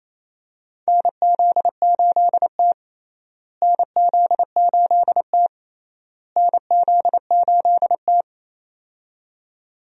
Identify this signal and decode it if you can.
Morse code, heard as a keyed tone, about 28 wpm, 700 Hz. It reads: NZ8T NZ8T NZ8T
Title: USA Call Signs - 4 Characters - Repeated 3x - 28wpm